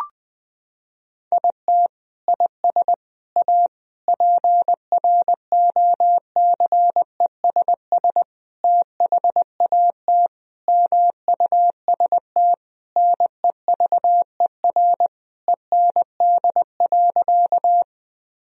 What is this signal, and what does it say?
Morse code, heard as a keyed tone, about 20 wpm, 700 Hz. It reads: IT IS A PROCESS THAT MUST NEVER END.
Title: Sentences from Top 1000 Words - Review Part 3 - 20wpm